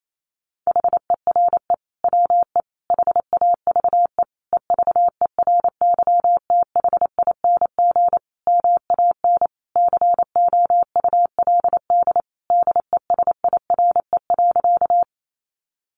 Morse code, heard as a keyed tone, about 28 wpm, 700 Hz. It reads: HERE WE HAVE EVERYTHING MAN COULD DESIRE.
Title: Sentences from Top 1100 Words 28wpm